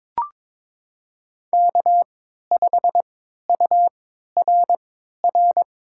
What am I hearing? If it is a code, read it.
Morse code, heard as a keyed tone, about 21 wpm, spaced as for 12 wpm, 700 Hz. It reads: K5URR
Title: USA Call Signs - 3x Character Spacing - 22wpm